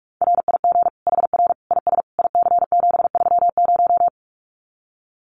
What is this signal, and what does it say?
Morse code, heard as a keyed tone, about 44 wpm, 700 Hz. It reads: RIG HR IS IC730
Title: Common QSO Phrases 45wpm